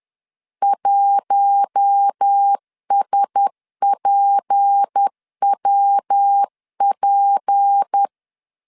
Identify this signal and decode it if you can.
Morse code, heard as a keyed tone, about 10 wpm, 800 Hz. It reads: 1SPWP
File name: LEZIONI / 6 PKX0 / 6 P 10wpm 2432